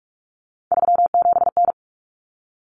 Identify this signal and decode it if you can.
Morse code, heard as a keyed tone, about 44 wpm, 700 Hz. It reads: 37N
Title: Common Cut Numbers - 3-Digit 45wpm